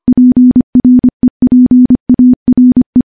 Morse code, heard as a keyed tone, about 25 wpm, 250 Hz. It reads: PREPARE
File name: morse / morse-training-25WPM